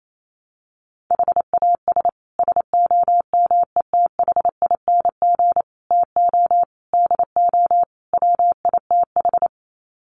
Morse code, heard as a keyed tone, about 28 wpm, 700 Hz. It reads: HAS SOMETHING TO DO WITH
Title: Sets of 5 Words Encore 28wpm